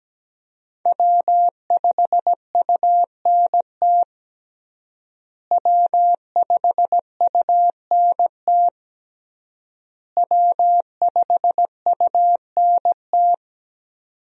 Morse code, heard as a keyed tone, about 17 wpm, 700 Hz. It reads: W5UNT W5UNT W5UNT
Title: USA Call Signs - 5 Characters - Repeated 3x - 17wpm